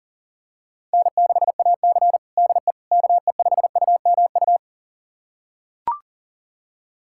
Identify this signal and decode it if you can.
Morse code, heard as a keyed tone, about 40 wpm, 700 Hz. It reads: N6AC DE KE5UMU
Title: USA Callsign DE Callsign 40wpm